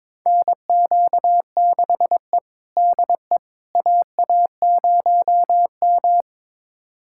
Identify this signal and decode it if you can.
Morse code, heard as a keyed tone, about 22 wpm, 700 Hz. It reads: NQ6E DE AA0M
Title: USA Callsign DE Callsign - 4 Characters - 22wpm